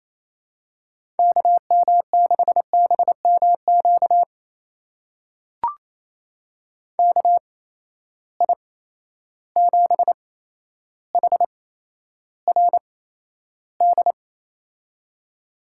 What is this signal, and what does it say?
Morse code, heard as a keyed tone, about 28 wpm, spaced as for 12 wpm, 700 Hz. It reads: KM6BMQ KI7HRD
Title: USA Call Signs - 6 Characters - 8x Character Spacing - 28wpm